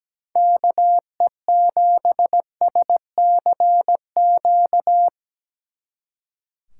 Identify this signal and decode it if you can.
Morse code, heard as a keyed tone, about 19 wpm, 700 Hz. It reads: KE7SCQ
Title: Speed Racing - USA 2x3 Call Signs 17wpm